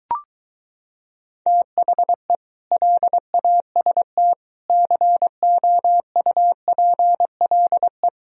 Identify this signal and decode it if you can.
Morse code, heard as a keyed tone, about 23 wpm, 700 Hz. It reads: THE LAST COUPLE